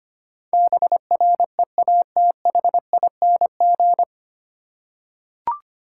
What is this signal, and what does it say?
Morse code, heard as a keyed tone, about 25 wpm, 700 Hz. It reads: BREATHING
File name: practice / 25wpm / 9 Letter Words - First 3 Letters Given - 25wpm